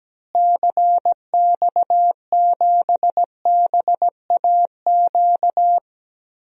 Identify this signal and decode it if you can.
Morse code, heard as a keyed tone, about 17 wpm, 700 Hz. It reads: CX7BAQ
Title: DXCC-100 Call Signs - 6 Characters - Repeated 2x - 17wpm